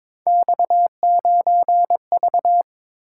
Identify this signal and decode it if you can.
Morse code, heard as a keyed tone, about 26 wpm, 700 Hz. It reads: X9V
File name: Speed-Racing - Three Letter-Number Combinations - 22wpm